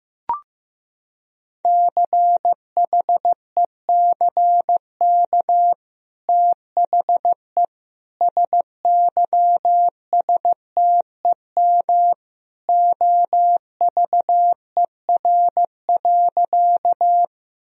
Morse code, heard as a keyed tone, about 15 wpm, 700 Hz. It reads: CHECK THE SYSTEM OVER.
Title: Sentences from Top 1000 Words - Review Part 3 - 15wpm